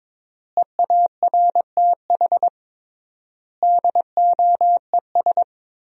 Morse code, heard as a keyed tone, about 21 wpm, 700 Hz. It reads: EARTH DOES
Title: 3x Word-Spacing - Sentences from Top 200 Words - 22wpm